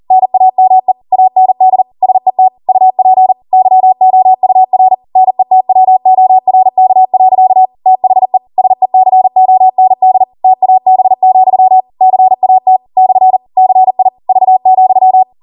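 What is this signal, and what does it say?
Morse code, heard as a keyed tone, about 40 wpm, 750 Hz. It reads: NAME AND SET UP YOUR NETWORK. THE SECOND TAB, CAT / CI V,
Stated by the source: Code practice at 40 wpm.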